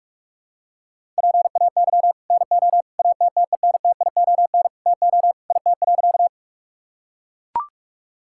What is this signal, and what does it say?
Morse code, heard as a keyed tone, about 44 wpm, 700 Hz. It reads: PAY NO ATTENTION TO IT.
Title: Sentences from Top 1000 Words - Review Part 6 - 45wpm